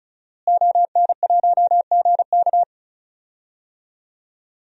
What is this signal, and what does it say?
Morse code, heard as a keyed tone, about 36 wpm, 700 Hz. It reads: ON1GK
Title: DXCC-100 Call Signs - 5 Characters - Repeated 2x - 35wpm